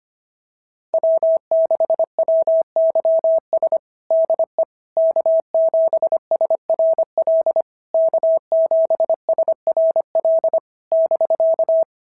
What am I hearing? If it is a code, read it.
Morse code, heard as a keyed tone, about 25 wpm, 650 Hz. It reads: W6WYS DE K7SRL K7SRL <BK>
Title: SOTA QSO Practice 25wpm